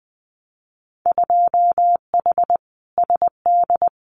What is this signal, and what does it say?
Morse code, heard as a keyed tone, about 20 wpm, 700 Hz. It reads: 2H SD
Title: Winter Field Day Exchanges 20wpm